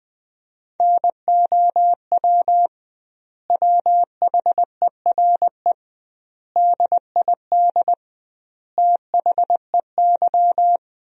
Morse code, heard as a keyed tone, about 20 wpm, 700 Hz. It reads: NOW WHERE DID THEY
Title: 2x Word-Spacing - Sentences from Top 200 Words - 20wpm